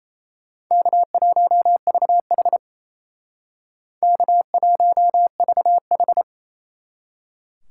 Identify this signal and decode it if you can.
Morse code, heard as a keyed tone, about 35 wpm, 700 Hz. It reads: K1VH K1VH
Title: Speed Racing - USA 1x2 Call Signs 28wpm